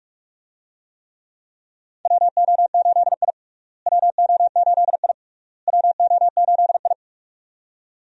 Morse code, heard as a keyed tone, about 46 wpm, 700 Hz. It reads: WO8I WO8I WO8I
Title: USA Call Signs - 4 Characters - Repeated 3x - 45wpm